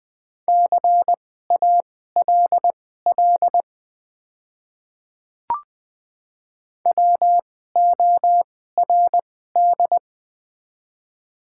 Morse code, heard as a keyed tone, about 20 wpm, spaced as for 14 wpm, 700 Hz. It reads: CALL WORD
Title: 2x Character Spacing - Top 100 Words - 20wpm